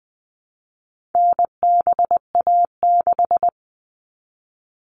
Morse code, heard as a keyed tone, about 20 wpm, 700 Hz. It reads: NBA6
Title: All Cut Numbers - 4-Digit 20wpm